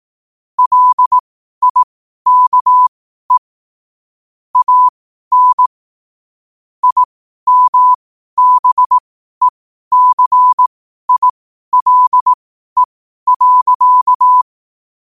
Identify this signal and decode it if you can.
Morse code, heard as a keyed tone, about 18 wpm, spaced as for 12 wpm, 1000 Hz. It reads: LIKE AN IMBECILE.
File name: qotdfiles / 12wpm